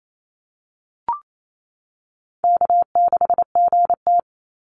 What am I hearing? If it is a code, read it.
Morse code, heard as a keyed tone, about 28 wpm, 700 Hz. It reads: K6GT
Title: Easy Call Signs - 28wpm